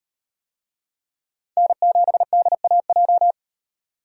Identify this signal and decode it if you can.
Morse code, heard as a keyed tone, about 38 wpm, 700 Hz. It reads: N7DAJ